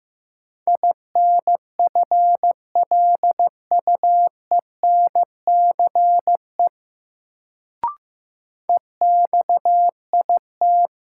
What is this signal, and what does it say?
Morse code, heard as a keyed tone, about 15 wpm, 700 Hz. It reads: INFLUENCE EXIT
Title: Sentences from Top 1100 Words - New Word Drill - 15wpm